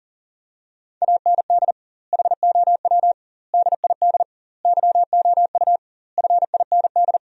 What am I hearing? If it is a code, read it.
Morse code, heard as a keyed tone, about 40 wpm, 700 Hz. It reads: AND HOW DID YOU FIND
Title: 2x Word-Spacing - Sentences from Top 200 Words - 40wpm